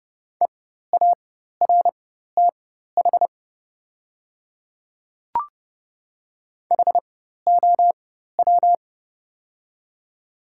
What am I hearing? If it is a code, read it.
Morse code, heard as a keyed tone, about 30 wpm, spaced as for 14 wpm, 700 Hz. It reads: EARTH HOW
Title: 4x Character Spacing - Top 200 Words - 30wpm